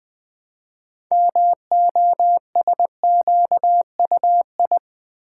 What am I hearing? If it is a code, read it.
Morse code, heard as a keyed tone, about 20 wpm, 700 Hz. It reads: MOSQUI
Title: Build A Word - 2 Letters At A Time - 8-Letter Words - 20wpm